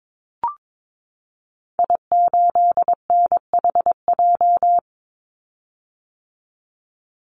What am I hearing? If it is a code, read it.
Morse code, heard as a keyed tone, about 22 wpm, 700 Hz. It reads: I8NHJ
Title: Call Signs - 22wpm